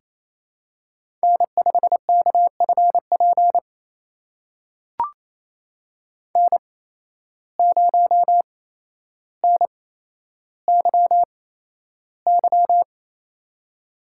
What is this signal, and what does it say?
Morse code, heard as a keyed tone, about 29 wpm, spaced as for 12 wpm, 700 Hz. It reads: N5KFP N0NYY
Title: USA Call Signs - 5 Characters - 8x Character Spacing - 28wpm